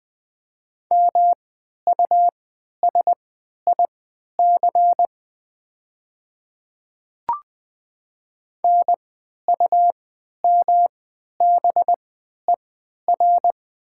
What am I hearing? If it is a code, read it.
Morse code, heard as a keyed tone, about 20 wpm, spaced as for 11 wpm, 700 Hz. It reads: MUSIC NUMBER
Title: 3x Character Spacing - Top 300 Words - 20wpm